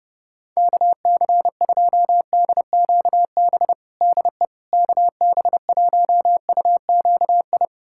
Morse code, heard as a keyed tone, about 32 wpm, 700 Hz. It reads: KC2DQB DE KB1UQI
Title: USA Callsign DE Callsign 30wpm